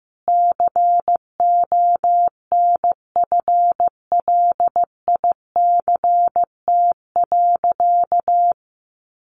Morse code, heard as a keyed tone, about 14 wpm, 700 Hz. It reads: CONFLICT.
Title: Sentences from Top 1300 Words 15wpm